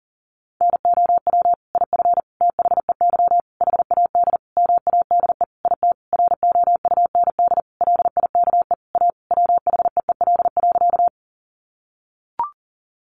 Morse code, heard as a keyed tone, about 40 wpm, 700 Hz. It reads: NOW IF THEY HAD MADE IT ROUND LIKE A WHEEL.
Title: Sentences from Top 500 Words 40wpm